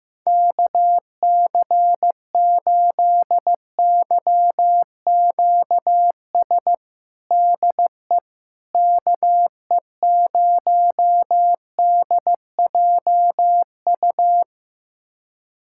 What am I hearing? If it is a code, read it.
Morse code, heard as a keyed tone, about 15 wpm, 700 Hz. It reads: KC8YQS DE KE0DJU
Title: USA Callsign DE Callsign 15wpm